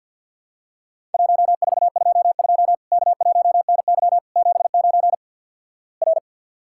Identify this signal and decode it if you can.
Morse code, heard as a keyed tone, about 50 wpm, 700 Hz. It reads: 1422 K1NY 79 R
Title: Worked All Europe DX Contest - QTC Exchange with Repeats - 50wpm